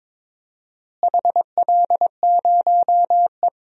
Morse code, heard as a keyed tone, about 21 wpm, 700 Hz. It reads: HL0E
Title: DXCC-100 Call Signs - 4 Characters - Repeated 2x - 22wpm